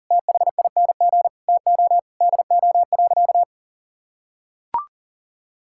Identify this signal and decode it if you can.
Morse code, heard as a keyed tone, about 40 wpm, 700 Hz. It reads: THING TO DO.
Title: Sentences from Top 1000 Words - Review Part 5 - 40wpm